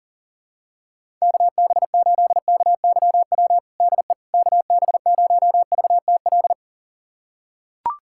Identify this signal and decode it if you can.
Morse code, heard as a keyed tone, about 37 wpm, 700 Hz. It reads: KB8KYW DE KB0VTL
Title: USA Callsign DE Callsign 40wpm